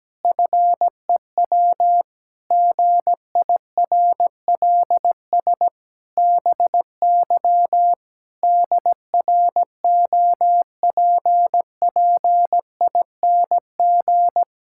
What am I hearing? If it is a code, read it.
Morse code, heard as a keyed tone, about 17 wpm, 700 Hz. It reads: FEW GIRLS BY DROPPING
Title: Pangram Challenge v2 - 17wpm